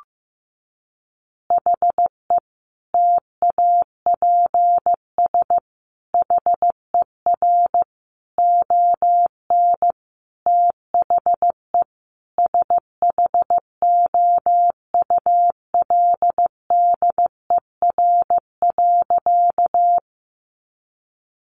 Morse code, heard as a keyed tone, about 15 wpm, 700 Hz. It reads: HE TAPS HER ON THE SHOULDER.
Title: Additional Easy Phrases 15wpm